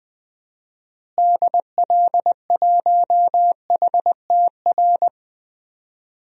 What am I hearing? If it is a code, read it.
Morse code, heard as a keyed tone, about 20 wpm, 700 Hz. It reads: DL1HTR
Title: DXCC-100 Call Signs - 6 Characters - Repeated 2x - 20wpm